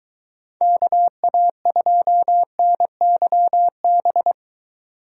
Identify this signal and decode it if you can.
Morse code, heard as a keyed tone, about 23 wpm, 700 Hz. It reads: KA2NYB